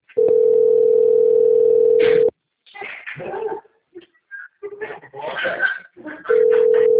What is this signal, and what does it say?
Ringback tone (North America), 440 + 480 Hz.